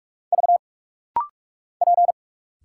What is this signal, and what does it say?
Morse code, heard as a keyed tone, about 44 wpm, 700 Hz. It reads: V P
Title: Single Letter-Number Alt - Rapid-Fire 45wpm